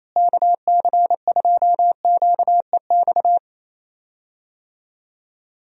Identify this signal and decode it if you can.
Morse code, heard as a keyed tone, about 28 wpm, 700 Hz. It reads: KC2QEX
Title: USA Call Signs - 6 Characters - Repeated 2x - 28wpm